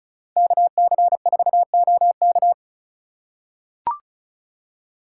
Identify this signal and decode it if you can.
Morse code, heard as a keyed tone, about 34 wpm, 700 Hz. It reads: KC4OK
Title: USA Call Signs - 5 Characters - 4x Character Spacing - 35wpm